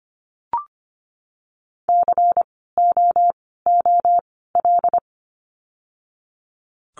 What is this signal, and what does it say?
Morse code, heard as a keyed tone, about 25 wpm, spaced as for 16 wpm, 700 Hz. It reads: COOL
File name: course / 25:10wpm / 094 Lesson 16f - Letter c - Words 25wpm